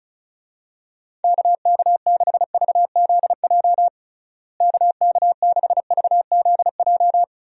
Morse code, heard as a keyed tone, about 36 wpm, 700 Hz. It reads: KK6VZJ KK6VZJ
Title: USA Call Signs - 6 Characters - Repeated 2x - 35wpm